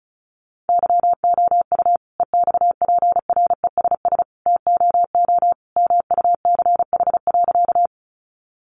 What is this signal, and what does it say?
Morse code, heard as a keyed tone, about 34 wpm, 700 Hz. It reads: YOU EXPRESS TOO MUCH.
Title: Sentences from Top 1100 Words 35wpm